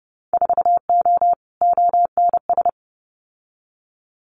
Morse code, heard as a keyed tone, about 30 wpm, 700 Hz. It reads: 4O ONS
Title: Winter Field Day Exchanges 30wpm